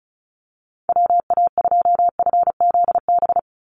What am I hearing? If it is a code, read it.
Morse code, heard as a keyed tone, about 34 wpm, 700 Hz. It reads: WA2FZB
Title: Call Signs - 35wpm